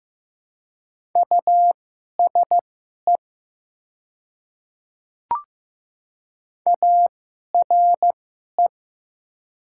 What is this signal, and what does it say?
Morse code, heard as a keyed tone, about 16 wpm, spaced as for 11 wpm, 700 Hz. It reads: USE ARE
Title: 2x Character Spacing - Top 100 Words - 15wpm